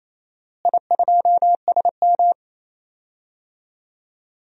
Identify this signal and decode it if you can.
Morse code, heard as a keyed tone, about 28 wpm, 700 Hz. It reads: I2SM
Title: DXCC-100 Call Signs - 4 Characters - Repeated 3x - 28wpm